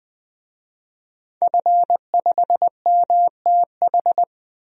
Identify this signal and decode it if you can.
Morse code, heard as a keyed tone, about 20 wpm, 700 Hz. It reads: F5MTH
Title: DXCC-100 Call Signs - 5 Characters - Repeated 2x - 20wpm